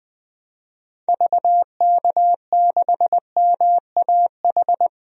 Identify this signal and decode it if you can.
Morse code, heard as a keyed tone, about 20 wpm, 700 Hz. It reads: VK6MAH
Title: DXCC-100 Call Signs - 6 Characters - 20wpm